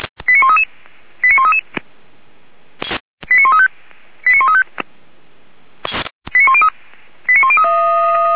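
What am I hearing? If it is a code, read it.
Five-tone selective call (ZVEI), heard as quick runs of five tones, tones of about 70 ms. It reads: 89133 89133 89135 89135 89103
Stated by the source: ZVEI five-tone selective call 89132 received by SDR, fire brigade Feuerwehr Orlamünde.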